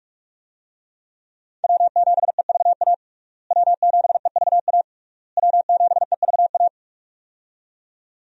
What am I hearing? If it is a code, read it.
Morse code, heard as a keyed tone, about 41 wpm, 700 Hz. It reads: W7EVA W7EVA W7EVA
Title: USA Call Signs - 5 Characters - Repeated 3x - 45wpm